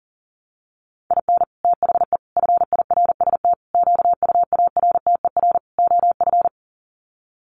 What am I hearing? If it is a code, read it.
Morse code, heard as a keyed tone, about 40 wpm, 700 Hz. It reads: IN THE FIRST QUARTER OF
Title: Sets of 5 Words Encore2 40wpm